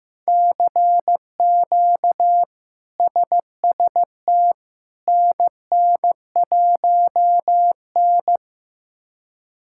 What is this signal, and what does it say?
Morse code, heard as a keyed tone, about 15 wpm, 700 Hz. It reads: CQ SST NN1N
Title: K1USN SST QSO Practice - 15wpm